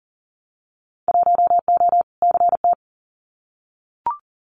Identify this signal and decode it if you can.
Morse code, heard as a keyed tone, about 40 wpm, 700 Hz. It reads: 1O CT
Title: Winter Field Day Exchanges 40wpm